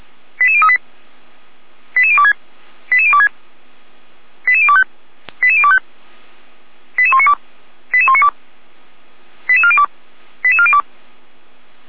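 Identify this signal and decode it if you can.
Five-tone selective call (ZVEI), heard as quick runs of five tones, tones of about 70 ms. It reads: 88038 88026 88026 88025 88025 88192 88192 88402 88402
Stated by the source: ZVEI five-tone selective call 88102 received by SDR, logged by Feuerwehr Orlamünde.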